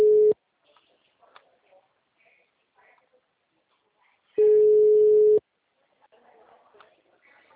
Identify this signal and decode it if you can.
Ringback tone (Europe, 425 Hz), with speech.